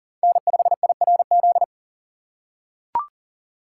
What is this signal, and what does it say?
Morse code, heard as a keyed tone, about 40 wpm, 700 Hz. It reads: N5IRZ
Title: USA Call Signs - 5 Characters - 3x Character Spacing - 40wpm